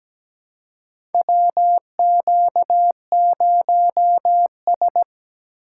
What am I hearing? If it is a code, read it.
Morse code, heard as a keyed tone, about 17 wpm, 700 Hz. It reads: WQ0S
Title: USA Call Signs - 4 Characters - 2x Character Spacing - 17wpm